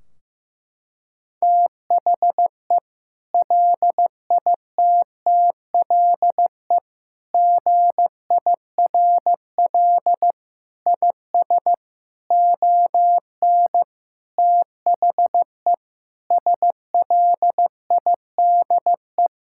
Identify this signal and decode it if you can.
Morse code, heard as a keyed tone, about 15 wpm, 700 Hz. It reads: THE LITTLE GIRL IS ON THE SLIDE
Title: Sentences from Top 1600 Words 15wpm